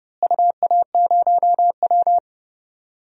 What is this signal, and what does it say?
Morse code, heard as a keyed tone, about 30 wpm, 700 Hz. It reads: UA0W
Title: DXCC-100 Call Signs - 4 Characters - Repeated 2x - 30wpm